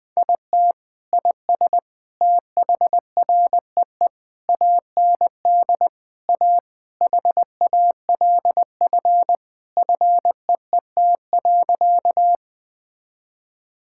Morse code, heard as a keyed tone, about 21 wpm, 700 Hz. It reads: IT IS THREE AND A HALF FEET.
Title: Sentences with 3-Word Phrases - 20wpm